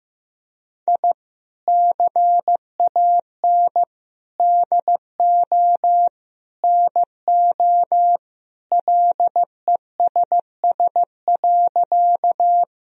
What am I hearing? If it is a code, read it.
Morse code, heard as a keyed tone, about 15 wpm, 700 Hz. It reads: I CAN DO NO LESS.
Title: Sentences from Top 1000 Words - Review Part 2 - 15wpm